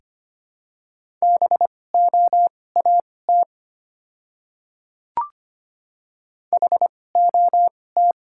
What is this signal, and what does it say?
Morse code, heard as a keyed tone, about 25 wpm, spaced as for 18 wpm, 700 Hz. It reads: BOAT HOT
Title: 2x Character Spacing - Top 500 Words - 25wpm